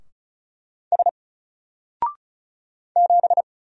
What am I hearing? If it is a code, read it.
Morse code, heard as a keyed tone, about 36 wpm, 700 Hz. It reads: S 7
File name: Single-Letter-Number-35wpm